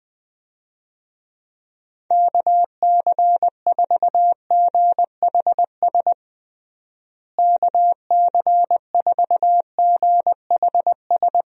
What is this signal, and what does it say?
Morse code, heard as a keyed tone, about 20 wpm, 700 Hz. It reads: KC4GHS KC4GHS
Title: USA Call Signs - 6 Characters - Repeated 2x - 20wpm